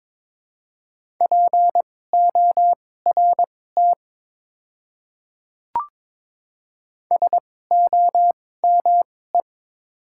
Morse code, heard as a keyed tone, about 22 wpm, spaced as for 16 wpm, 700 Hz. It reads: PORT SOME
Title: 2x Character Spacing - Top 200 Words - 22wpm